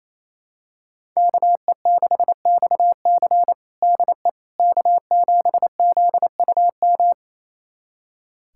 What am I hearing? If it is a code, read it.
Morse code, heard as a keyed tone, about 28 wpm, 700 Hz. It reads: KE6XC DE K7ZUM
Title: USA Callsign DE Callsign - 5 Characters - 28wpm